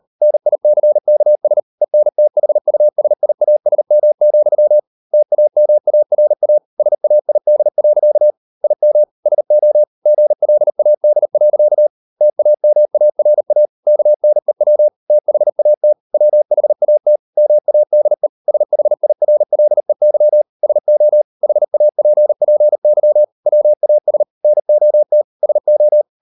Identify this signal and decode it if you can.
Morse code, heard as a keyed tone, about 40 wpm, 600 Hz. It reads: NICKS ENTHUSIASM, TAMARA SAID. IM SO GLAD. TAMARA KNEW THAT WHAT MADE SHIRLEY SO HAPPY WAS NOT SO